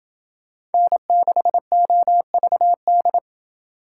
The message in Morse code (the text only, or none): N6OVD